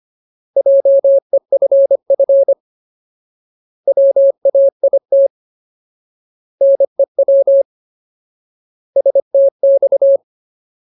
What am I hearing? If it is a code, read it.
Morse code, heard as a keyed tone, about 25 wpm, 550 Hz. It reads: JEFF WAIT NEW STX
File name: Ultimate Word List / 25WPM Ultimate